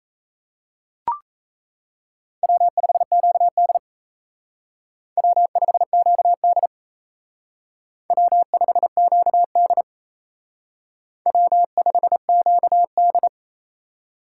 Morse code, heard as a keyed tone, about 35 wpm, 700 Hz. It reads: W5QD W5QD W5QD W5QD
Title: Speed Racing - USA 1x2 Call Signs 28wpm